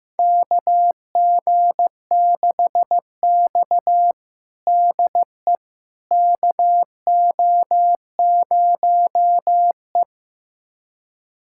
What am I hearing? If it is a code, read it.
Morse code, heard as a keyed tone, about 16 wpm, 700 Hz. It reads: KG6X DE KO0E
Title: USA Callsign DE Callsign - 4 Characters - 15wpm